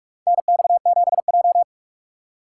Morse code, heard as a keyed tone, about 45 wpm, 700 Hz. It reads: NX7J
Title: USA Call Signs - 4 Characters - 3x Character Spacing - 45wpm